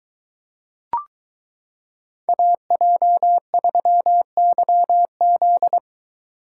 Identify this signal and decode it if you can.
Morse code, heard as a keyed tone, about 22 wpm, 700 Hz. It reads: AJ3YZ